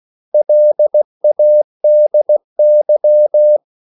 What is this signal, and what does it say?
Morse code, heard as a keyed tone, about 16 wpm, 600 Hz. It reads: LADY
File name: CW / Practice / 500MostCommonEnglishWords16